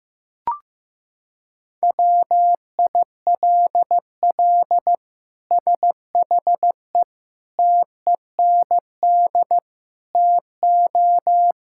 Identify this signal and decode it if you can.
Morse code, heard as a keyed tone, about 15 wpm, 700 Hz. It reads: WILL SHE TEND TO
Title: Sentences from Top 1000 Words - Review Part 5 - 15wpm